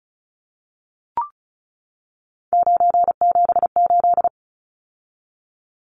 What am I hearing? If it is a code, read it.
Morse code, heard as a keyed tone, about 34 wpm, 700 Hz. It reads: 978
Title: Numbers - 3-Digit 35wpm